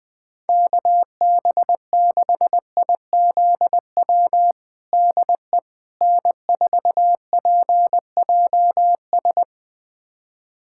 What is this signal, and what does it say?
Morse code, heard as a keyed tone, about 20 wpm, 700 Hz. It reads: KB6IZW DE N4PJS
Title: USA Callsign DE Callsign 20wpm